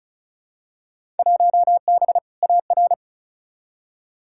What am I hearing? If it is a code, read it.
Morse code, heard as a keyed tone, about 36 wpm, 700 Hz. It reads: 1B AR
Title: Field Day Exchanges Encore - 35wpm